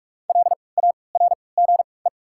Morse code, heard as a keyed tone, about 44 wpm, 690 Hz. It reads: L A R G E
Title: 3x Character Spacing - Top 300 Words - 45wpm